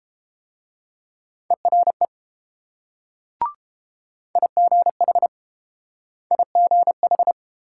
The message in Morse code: ERE IGH IGH

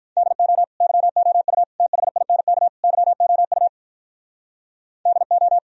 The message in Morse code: DO YOU THINK YOU DO

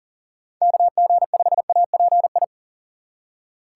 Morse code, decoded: KG5API

